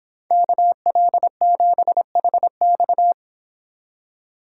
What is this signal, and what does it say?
Morse code, heard as a keyed tone, about 25 wpm, 700 Hz. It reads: KL7HX